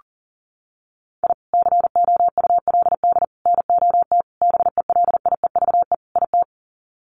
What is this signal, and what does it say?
Morse code, heard as a keyed tone, about 40 wpm, 700 Hz. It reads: I COULD NOT BELIEVE IT
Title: Sets of 5 Words Encore 40wpm